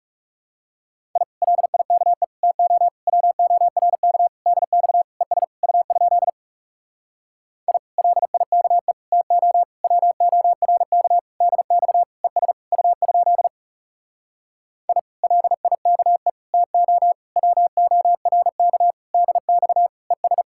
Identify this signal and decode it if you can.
Morse code, heard as a keyed tone, about 40 wpm, 700 Hz. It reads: I LIKE TO WORK DX ES U? I LIKE TO WORK DX ES U? I LIKE TO WORK DX ES